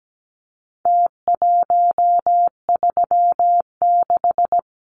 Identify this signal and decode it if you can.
Morse code, heard as a keyed tone, about 17 wpm, 700 Hz. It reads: T136
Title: Numbers - Contest Style - 4-Digit 17wpm